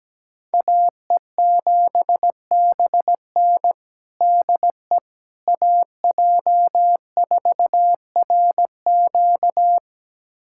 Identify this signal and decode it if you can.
Morse code, heard as a keyed tone, about 17 wpm, 700 Hz. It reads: AE7BN DE AJ4RQ
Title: USA Callsign DE Callsign - 5 Characters - 17wpm